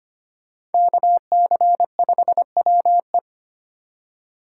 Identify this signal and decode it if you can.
Morse code, heard as a keyed tone, about 32 wpm, 700 Hz. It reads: KC5WE